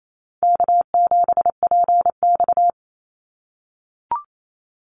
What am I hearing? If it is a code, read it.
Morse code, heard as a keyed tone, about 28 wpm, 700 Hz. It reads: K7PX
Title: NA SOTA Activator Callsigns 28wpm